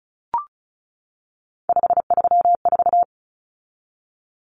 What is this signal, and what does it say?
Morse code, heard as a keyed tone, about 34 wpm, 700 Hz. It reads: 534
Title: Common Cut Numbers - 3-Digit 35wpm